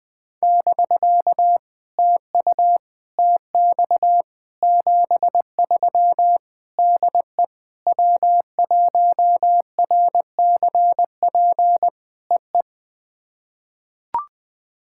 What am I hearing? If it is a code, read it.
Morse code, heard as a keyed tone, about 20 wpm, 700 Hz. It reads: <BK> TU TX 73 DE W1RCP EE
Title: POTA QSO Practice - 20wpm